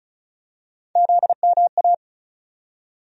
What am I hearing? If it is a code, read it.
Morse code, heard as a keyed tone, about 36 wpm, 700 Hz. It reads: ZMA
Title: USA Call Sign Postfixes - 3 Letters 35wpm